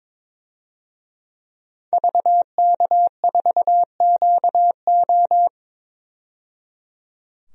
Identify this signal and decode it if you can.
Morse code, heard as a keyed tone, about 22 wpm, 700 Hz. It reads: VK4QO
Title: DXCC-100 Call Signs - 5 Characters - 22wpm